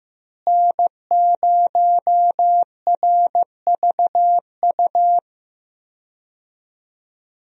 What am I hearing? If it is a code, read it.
Morse code, heard as a keyed tone, about 15 wpm, 700 Hz. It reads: N0RVU
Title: USA Call Signs - 5 Characters - Repeated 3x - 15wpm